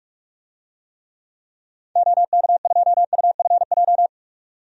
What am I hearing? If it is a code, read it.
Morse code, heard as a keyed tone, about 44 wpm, 700 Hz. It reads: OK2UFJ
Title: DXCC-100 Call Signs - 6 Characters - 45wpm